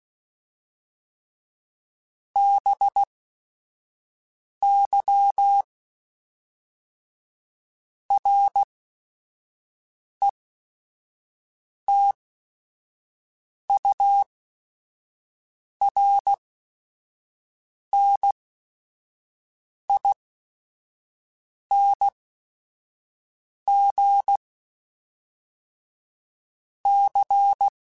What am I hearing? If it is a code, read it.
Morse code, heard as a keyed tone, about 16 wpm, spaced as for 5 wpm, 800 Hz. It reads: BY RETURNING C
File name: Technical Subjects / Technical Subjects-mp3s / XC @5